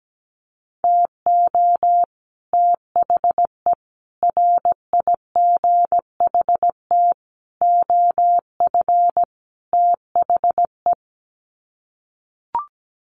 Morse code, heard as a keyed tone, about 17 wpm, 700 Hz. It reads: TO THE RIGHT OF THE
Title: Sets of 5 Words Encore 17wpm